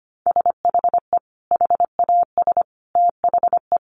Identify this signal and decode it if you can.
Morse code, heard as a keyed tone, about 25 wpm, 700 Hz. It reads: SHE HAS THE
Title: Sentences from Top 800 Words 25wpm